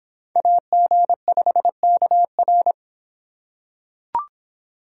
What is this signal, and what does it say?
Morse code, heard as a keyed tone, about 26 wpm, 700 Hz. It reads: AG5KR